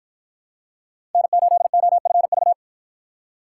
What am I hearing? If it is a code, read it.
Morse code, heard as a keyed tone, about 53 wpm, 700 Hz. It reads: N8OFV